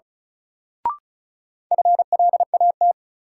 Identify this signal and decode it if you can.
Morse code, heard as a keyed tone, about 36 wpm, 700 Hz. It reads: FLAT
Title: Sentences from Top 1000 Words - New Word Drill - 35wpm